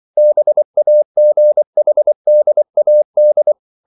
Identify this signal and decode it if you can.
Morse code, heard as a keyed tone, about 24 wpm, 600 Hz. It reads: BAGHDAD
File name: CW / Practice / NationalCapitals24